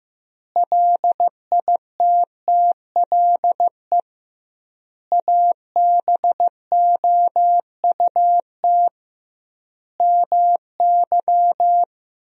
Morse code, heard as a keyed tone, about 16 wpm, 700 Hz. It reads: LITTLE ABOUT MY
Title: 2x Word-Spacing - Sentences from Top 400 Words - 15wpm